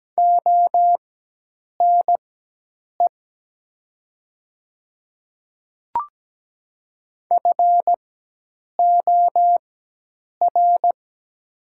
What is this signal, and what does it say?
Morse code, heard as a keyed tone, about 17 wpm, spaced as for 8 wpm, 700 Hz. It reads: ONE FOR
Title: 4x Character Spacing - Top 100 Words - 17wpm